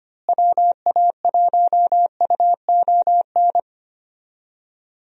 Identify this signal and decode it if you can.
Morse code, heard as a keyed tone, about 25 wpm, 700 Hz. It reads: WA1UON
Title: USA Call Signs - 6 Characters - Repeated 2x - 25wpm